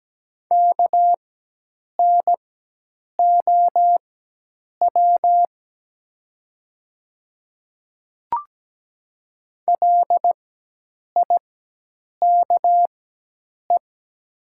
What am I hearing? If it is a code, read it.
Morse code, heard as a keyed tone, about 17 wpm, spaced as for 8 wpm, 700 Hz. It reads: KNOW LIKE
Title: 4x Character Spacing - Top 100 Words - 17wpm